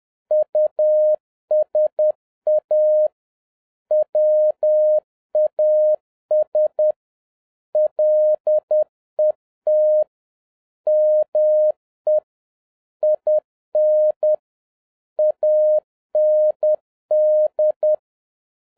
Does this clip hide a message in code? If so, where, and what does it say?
Yes: Morse code from 0.3 to 17.9 s, heard as a keyed tone, about 10 wpm, 610 Hz. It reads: USA WAS LET ME IN AND